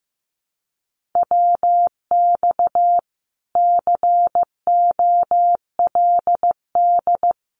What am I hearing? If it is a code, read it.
Morse code, heard as a keyed tone, about 15 wpm, 700 Hz. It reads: WX COLD
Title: Common QSO Phrases Encore 15wpm